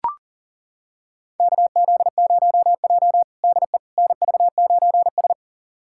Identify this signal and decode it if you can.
Morse code, heard as a keyed tone, about 40 wpm, 700 Hz. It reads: KZ0J DE NV9S
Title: USA Callsign DE Callsign - 4 Characters - 40wpm